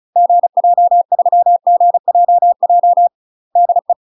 Morse code, heard as a keyed tone, about 34 wpm, 700 Hz. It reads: GJ3GJJ DE
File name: UK MASTER QSOs TRAINING / 35WPM / UK MASTER QSOs 0015